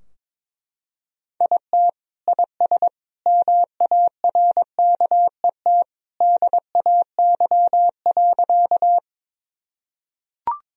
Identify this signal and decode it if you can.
Morse code, heard as a keyed tone, about 22 wpm, 700 Hz. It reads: IT IS MARKET DAY.
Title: Sentences from Top 1000 Words - Review Part 3 - 22wpm